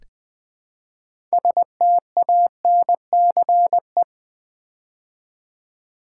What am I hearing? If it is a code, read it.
Morse code, heard as a keyed tone, about 20 wpm, 700 Hz. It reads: STANCE
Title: Build A Word - 2 Letters At A Time - 6-Letter Words - 20wpm